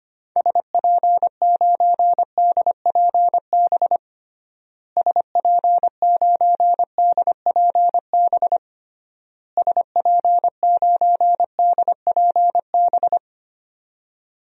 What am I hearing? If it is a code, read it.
Morse code, heard as a keyed tone, about 25 wpm, 700 Hz. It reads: SP9DPB SP9DPB SP9DPB
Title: DXCC-100 Call Signs - 6 Characters - Repeated 3x - 25wpm